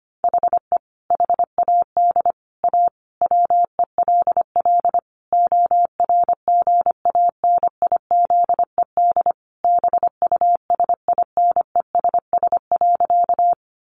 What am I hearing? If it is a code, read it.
Morse code, heard as a keyed tone, about 25 wpm, 700 Hz. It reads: HE HAD A WELL ORGANIZED BUSINESS.
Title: Sentences from Top 1300 Words 25wpm